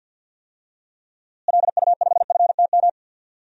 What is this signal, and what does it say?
Morse code, heard as a keyed tone, about 53 wpm, 700 Hz. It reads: LU5FTM
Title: DXCC-100 Call Signs - 6 Characters - Repeated 2x - 50wpm